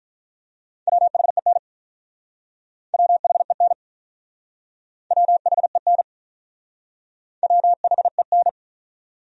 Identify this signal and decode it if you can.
Morse code, heard as a keyed tone, about 47 wpm, 700 Hz. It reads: WHEN WHEN WHEN WHEN